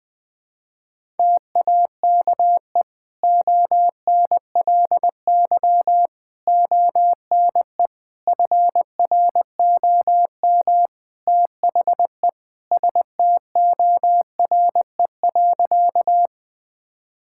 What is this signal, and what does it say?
Morse code, heard as a keyed tone, about 20 wpm, 700 Hz. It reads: TAKE ONLY ONE FROM THE STORE.
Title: Sentences from Top 1000 Words - Review Part 5 - 20wpm